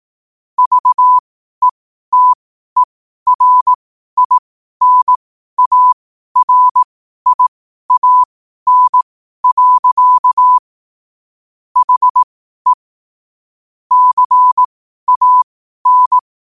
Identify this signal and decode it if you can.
Morse code, heard as a keyed tone, about 18 wpm, spaced as for 12 wpm, 1000 Hz. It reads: VETERINARIAN. HE CAN
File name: qotdfiles / 12wpm